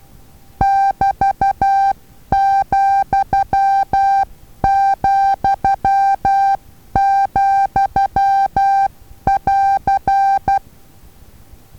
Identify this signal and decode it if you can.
Morse code, heard as a keyed tone, about 12 wpm, 790 Hz. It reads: <BT>,,,<AR>
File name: Taviro HA6VA A